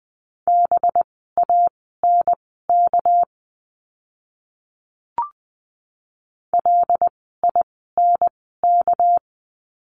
Morse code, heard as a keyed tone, about 20 wpm, spaced as for 14 wpm, 700 Hz. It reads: BANK LINK
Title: 199 Lesson 32f - Letter k - Words 20wpm